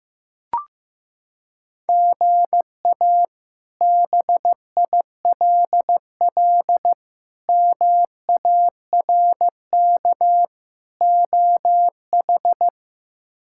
Morse code, heard as a keyed tone, about 16 wpm, 700 Hz. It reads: GA BILL MARK OH
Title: K1USN SST QSO Practice - 15wpm